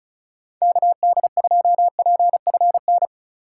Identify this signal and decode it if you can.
Morse code, heard as a keyed tone, about 36 wpm, 700 Hz. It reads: KD2PFN
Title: USA Call Signs - 6 Characters - 2x Character Spacing - 35wpm